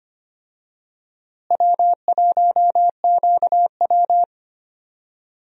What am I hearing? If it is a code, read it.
Morse code, heard as a keyed tone, about 25 wpm, 700 Hz. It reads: W1QW
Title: USA Call Signs - 4 Characters - Repeated 2x - 25wpm